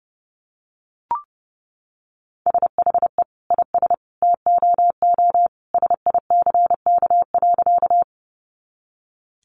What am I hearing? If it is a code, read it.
Morse code, heard as a keyed tone, about 30 wpm, 700 Hz. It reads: SHE IS TOO SICK.
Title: Easy Phrases 30wpm